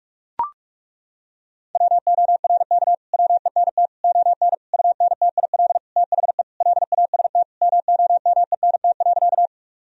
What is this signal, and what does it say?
Morse code, heard as a keyed tone, about 42 wpm, 700 Hz. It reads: WORK WENT ON UNTIL THE LAST MOMENT.
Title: Sentences from Top 1000 Words - Review Part 1 - 45wpm